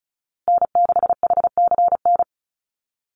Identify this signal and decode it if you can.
Morse code, heard as a keyed tone, about 34 wpm, 700 Hz. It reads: N6HCN
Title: Call Signs Repeated 2x - 35wpm